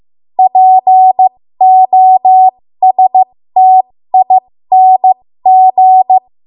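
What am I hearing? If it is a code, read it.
Morse code, heard as a keyed tone, about 15 wpm, spaced as for 13 wpm, 750 Hz. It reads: POSTING
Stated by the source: 13 wpm Morse code practice archive transmission.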